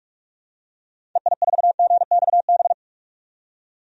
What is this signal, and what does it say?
Morse code, heard as a keyed tone, about 46 wpm, 700 Hz. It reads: EI4GXB
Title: DXCC-100 Call Signs - 6 Characters - 45wpm